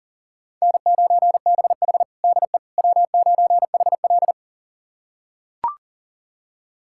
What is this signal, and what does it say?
Morse code, heard as a keyed tone, about 37 wpm, 700 Hz. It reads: N9BH DE W9HL
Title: USA Callsign DE Callsign - 4 Characters - 40wpm